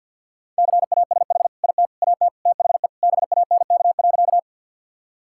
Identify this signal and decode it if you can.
Morse code, heard as a keyed tone, about 50 wpm, 700 Hz. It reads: CASH IT AT THE BANK.
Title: Sentences from Top 1000 Words - Review Part 5 - 50wpm